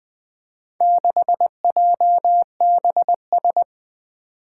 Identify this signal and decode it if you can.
Morse code, heard as a keyed tone, about 20 wpm, 700 Hz. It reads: 6JBS
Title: Four Challenging Letter-Number Combinations 20wpm